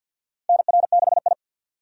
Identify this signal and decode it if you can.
Morse code, heard as a keyed tone, about 53 wpm, 700 Hz. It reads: NR6I
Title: USA Call Signs - 4 Characters - 50wpm